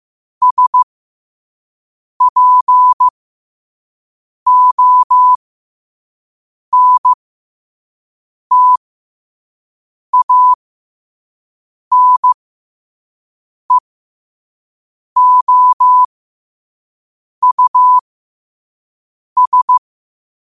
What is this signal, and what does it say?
Morse code, heard as a keyed tone, about 15 wpm, spaced as for 5 wpm, 1000 Hz. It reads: SPONTANEOUS
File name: qotdfiles / 5wpm